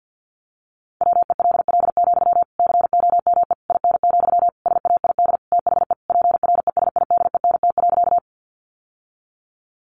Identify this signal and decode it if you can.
Morse code, heard as a keyed tone, about 50 wpm, 700 Hz. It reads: WELL, COME IN, SAID THE PRESIDENT.
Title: Sentences from Top 1000 Words 50wpm